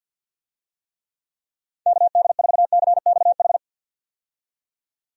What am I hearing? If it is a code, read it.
Morse code, heard as a keyed tone, about 47 wpm, 700 Hz. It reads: KD4CXH
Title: USA Call Signs - 6 Characters - 50wpm